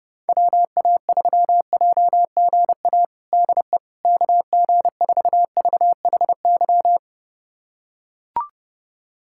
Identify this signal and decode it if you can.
Morse code, heard as a keyed tone, about 30 wpm, 690 Hz. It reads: WA3JGA DE KG4VHY
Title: USA Callsign DE Callsign 30wpm